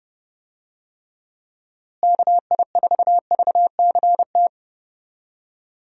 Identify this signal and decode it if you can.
Morse code, heard as a keyed tone, about 29 wpm, 700 Hz. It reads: KI4VCT
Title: USA Call Signs - 6 Characters - Repeated 3x - 30wpm